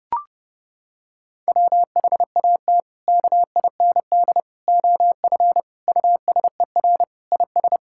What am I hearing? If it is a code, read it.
Morse code, heard as a keyed tone, about 32 wpm, 700 Hz. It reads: WHAT KIND OF USER IS
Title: Sentences from Top 1000 Words - Review Part 5 - 30wpm